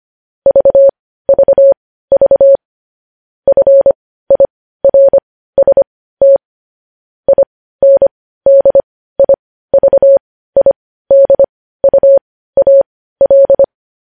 Morse code, heard as a keyed tone, about 25 wpm, spaced as for 15 wpm, 550 Hz. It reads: VVV FIRST INDIVIDUAL
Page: news headlines in morse code at 15 wpm